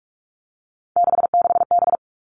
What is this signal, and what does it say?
Morse code, heard as a keyed tone, about 44 wpm, 700 Hz. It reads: 66B
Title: All Cut Numbers - 3-Digit 45wpm